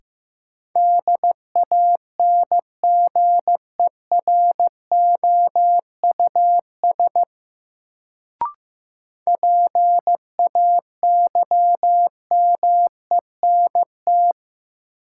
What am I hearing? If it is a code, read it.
Morse code, heard as a keyed tone, about 15 wpm, 700 Hz. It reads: DANGEROUS PAYMENT
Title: Sentences from Top 1200 Words - New Word Drill - 15wpm